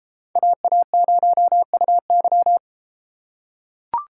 Morse code, heard as a keyed tone, about 33 wpm, 700 Hz. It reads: AA0UY